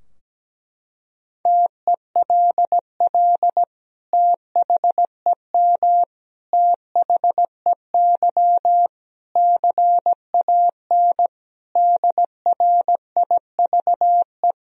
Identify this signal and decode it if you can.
Morse code, heard as a keyed tone, about 17 wpm, 700 Hz. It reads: TELL THEM THEY CAN DRIVE
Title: Sentences from Top 1000 Words - Review Part 2 - 17wpm